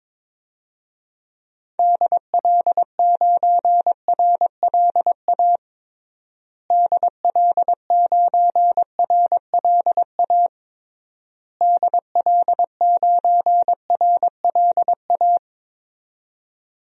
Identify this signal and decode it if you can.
Morse code, heard as a keyed tone, about 21 wpm, 700 Hz. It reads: DL9RLA DL9RLA DL9RLA
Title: DXCC-100 Call Signs - 6 Characters - Repeated 3x - 22wpm